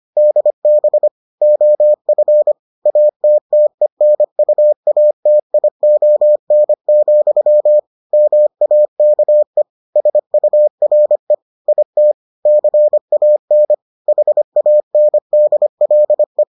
Morse code, heard as a keyed tone, about 25 wpm, 600 Hz. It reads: DB OF ATTENUATION, MAKE SURE IT CAN HANDLE